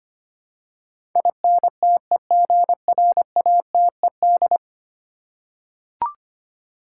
Morse code, heard as a keyed tone, about 25 wpm, 700 Hz. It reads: INTEGRATED
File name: 10 Letter Words - First 3 Letters Given - 25wpm